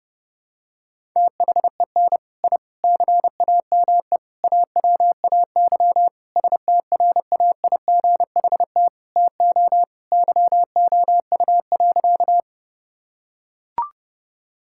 Morse code, heard as a keyed tone, about 30 wpm, 700 Hz. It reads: THEN I CAME AWAY STRAIGHT TO YOU.
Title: Sentences from Top 1000 Words - Review Part 6 - 30wpm